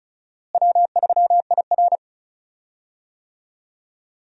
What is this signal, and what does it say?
Morse code, heard as a keyed tone, about 36 wpm, 700 Hz. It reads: W3IR
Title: USA Call Signs - 4 Characters - Repeated 3x - 35wpm